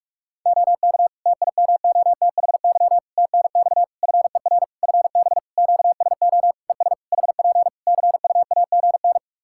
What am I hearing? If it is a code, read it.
Morse code, heard as a keyed tone, about 46 wpm, 700 Hz. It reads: OK TIMOTHY TNX FER FB QSO ES HP CUAGN
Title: K1USN SST QSO Practice 45wpm